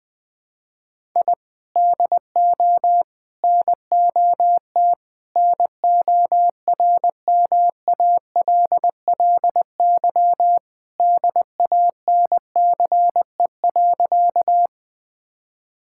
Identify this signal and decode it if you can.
Morse code, heard as a keyed tone, about 21 wpm, 700 Hz. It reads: I DO NOT NORMALLY DANCE.
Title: Sentences from Top 1000 Words - Review Part 6 - 20wpm